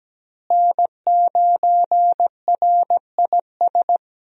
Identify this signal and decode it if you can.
Morse code, heard as a keyed tone, about 17 wpm, 700 Hz. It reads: N9RIS
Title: USA Call Signs - 5 Characters - Repeated 2x - 17wpm